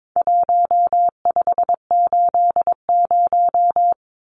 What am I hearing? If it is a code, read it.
Morse code, heard as a keyed tone, about 22 wpm, 700 Hz. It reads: 1580
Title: Numbers - Contest Style - 4-Digit 22wpm